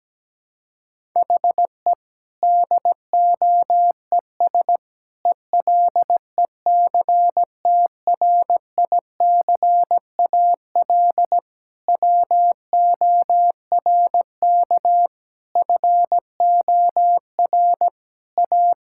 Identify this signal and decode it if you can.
Morse code, heard as a keyed tone, about 17 wpm, 700 Hz. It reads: HE DOES ELECTRICAL WORK FOR A
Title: Sentences from Top 1000 Words - Review Part 6 - 17wpm